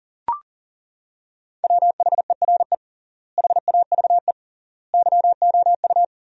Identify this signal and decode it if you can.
Morse code, heard as a keyed tone, about 40 wpm, 700 Hz. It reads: WHERE HAVE YOU
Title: 3x Word-Spacing - Sentences from Top 200 Words - 40wpm